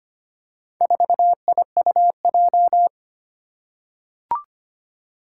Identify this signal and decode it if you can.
Morse code, heard as a keyed tone, about 25 wpm, 700 Hz. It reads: 4IUJ
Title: Four Challenging Letter-Number Combinations 25wpm